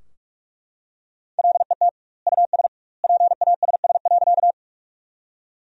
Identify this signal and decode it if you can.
Morse code, heard as a keyed tone, about 41 wpm, 700 Hz. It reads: LET US PASS.
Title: 2x Word-Spacing - Sentences from Top 400 Words - 45wpm